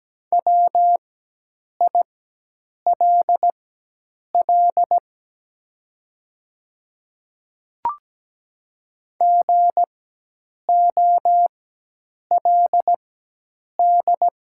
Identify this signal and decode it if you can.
Morse code, heard as a keyed tone, about 17 wpm, spaced as for 8 wpm, 700 Hz. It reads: WILL GOLD
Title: 4x Character Spacing - Top 500 Words - 17wpm